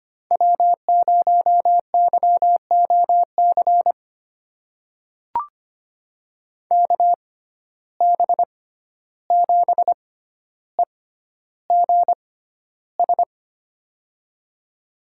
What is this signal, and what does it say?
Morse code, heard as a keyed tone, about 25 wpm, spaced as for 12 wpm, 700 Hz. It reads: W0YOC KB7EGS
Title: USA Call Signs - 6x Character Spacing - 25wpm